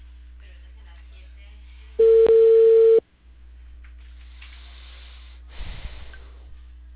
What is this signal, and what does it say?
Ringback tone (China, 450 Hz), with speech.